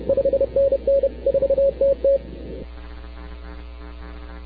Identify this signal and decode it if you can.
Morse code, heard as a keyed tone, about 30 wpm, 550 Hz. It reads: 5NN 4TT
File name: arrl05cw 20 k5tr